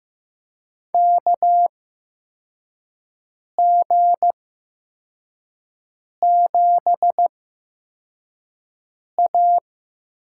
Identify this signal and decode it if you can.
Morse code, heard as a keyed tone, about 15 wpm, spaced as for 4 wpm, 700 Hz. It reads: KG7A
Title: USA Call Signs - 6 Characters - 8x Character Spacing - 15wpm